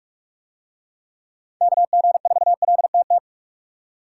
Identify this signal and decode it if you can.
Morse code, heard as a keyed tone, about 42 wpm, 700 Hz. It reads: KG4LTT